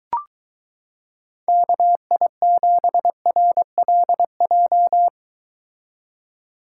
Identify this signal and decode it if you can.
Morse code, heard as a keyed tone, about 23 wpm, 700 Hz. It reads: KI7RLJ